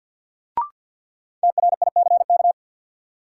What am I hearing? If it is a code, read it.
Morse code, heard as a keyed tone, about 52 wpm, 700 Hz. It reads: TRICK
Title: Sentences from Top 1500 Words - New Word Drill Alt - 50wpm